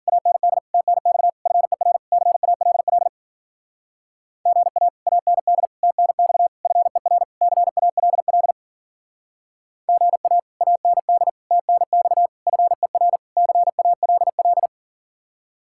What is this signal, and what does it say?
Morse code, heard as a keyed tone, about 52 wpm, 700 Hz. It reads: AND TNX FER CALL GA AND TNX FER CALL GA AND TNX FER CALL